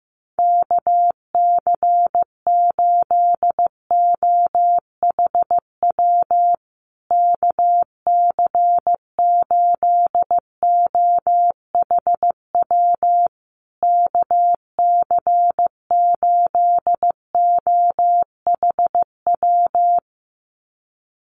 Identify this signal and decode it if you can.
Morse code, heard as a keyed tone, about 15 wpm, 700 Hz. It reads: KC8OHW KC8OHW KC8OHW
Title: USA Call Signs Repeated 3x 15wpm